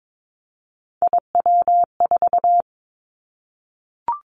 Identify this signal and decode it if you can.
Morse code, heard as a keyed tone, about 22 wpm, 700 Hz. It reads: IW4
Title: Callsign Prefixes for Top 25 DXCC - Repeated 3x - 22wpm